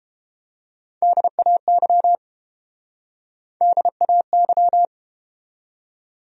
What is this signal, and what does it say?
Morse code, heard as a keyed tone, about 30 wpm, 700 Hz. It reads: DAY DAY